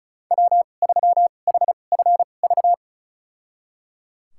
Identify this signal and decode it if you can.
Morse code, heard as a keyed tone, about 34 wpm, spaced as for 25 wpm, 700 Hz. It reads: W3HFV
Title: USA Call Signs - 5 Characters - 2x Character Spacing - 35wpm